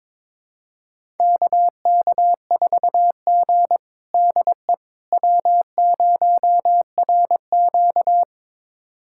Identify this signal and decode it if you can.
Morse code, heard as a keyed tone, about 22 wpm, 700 Hz. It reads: KK4G DE W0RQ
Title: USA Callsign DE Callsign - 4 Characters - 22wpm